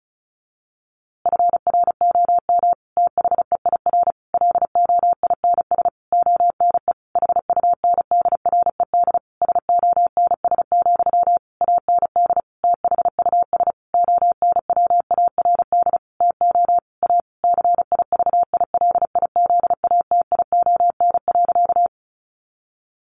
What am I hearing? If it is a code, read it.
Morse code, heard as a keyed tone, about 34 wpm, 700 Hz. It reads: FROM THEIR LOINS ONE HUNDRED SONS, AND THUS ONWARD TO A CIVILIZATION.
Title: A Sound Of Thunder 35wpm